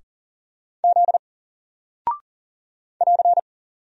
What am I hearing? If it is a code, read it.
Morse code, heard as a keyed tone, about 40 wpm, 700 Hz. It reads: Z <AR>
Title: Single Character (Letter:Number:Punctuation:Prosign) 40wpm